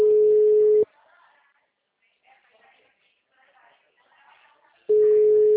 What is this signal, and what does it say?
Ringback tone (Europe, 425 Hz), with speech.
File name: out- -2011- - - .61970